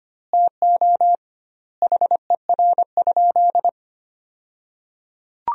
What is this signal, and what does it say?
Morse code, heard as a keyed tone, about 25 wpm, 700 Hz. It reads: TO HER?
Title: 2x Word-Spacing - Sentences from Top 200 Words - 25wpm